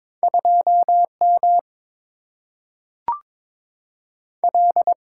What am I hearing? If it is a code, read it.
Morse code, heard as a keyed tone, about 22 wpm, 700 Hz. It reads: 2M L
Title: Single-Letter Single-Number 22wpm